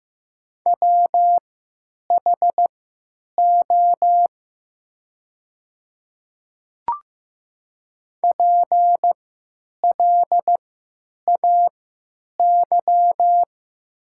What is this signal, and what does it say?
Morse code, heard as a keyed tone, about 15 wpm, spaced as for 9 wpm, 700 Hz. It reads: WHO PLAY
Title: 3x Character Spacing - Top 200 Words - 15wpm